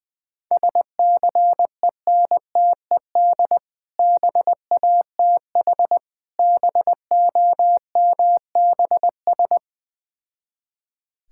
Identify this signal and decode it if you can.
Morse code, heard as a keyed tone, about 20 wpm, 700 Hz. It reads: SCENTED BATH BOMBS
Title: Difficult Words with Hints - 20wpm